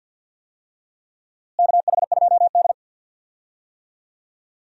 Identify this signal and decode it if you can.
Morse code, heard as a keyed tone, about 52 wpm, 700 Hz. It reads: KH2D
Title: USA Call Signs - 4 Characters - Repeated 2x - 50wpm